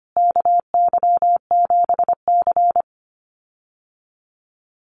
Morse code, heard as a keyed tone, about 25 wpm, 700 Hz. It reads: KY7C
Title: USA Call Signs Repeated 3x 25wpm